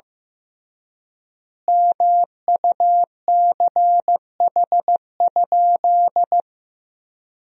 Morse code, heard as a keyed tone, about 16 wpm, 700 Hz. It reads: MUCH?
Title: 3x Word-Spacing - Sentences from Top 200 Words - 15wpm